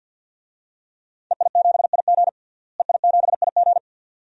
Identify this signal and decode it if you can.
Morse code, heard as a keyed tone, about 46 wpm, 700 Hz. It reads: EI7IG EI7IG
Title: DXCC-100 Call Signs - 5 Characters - Repeated 2x - 50wpm